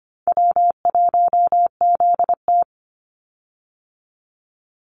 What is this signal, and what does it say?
Morse code, heard as a keyed tone, about 25 wpm, 700 Hz. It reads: W1ZT
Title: Easy Call Signs - 25wpm